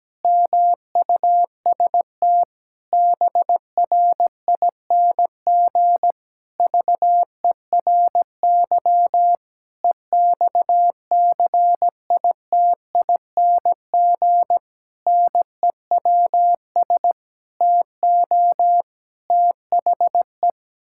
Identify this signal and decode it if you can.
Morse code, heard as a keyed tone, about 17 wpm, 700 Hz. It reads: MUST BRING VERY EXCITING NEWS TO THE
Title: Pangram Challenge v2 - 17wpm